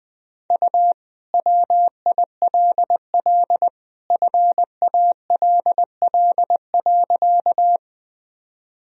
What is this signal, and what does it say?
Morse code, heard as a keyed tone, about 21 wpm, 700 Hz. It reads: U WILL FALL.
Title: Sentences from Top 500 Words and Abbreviations 20wpm